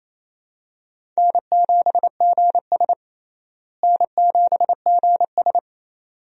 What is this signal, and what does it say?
Morse code, heard as a keyed tone, about 28 wpm, 700 Hz. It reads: N7GS N7GS
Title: USA Call Signs - 4 Characters - Repeated 2x - 28wpm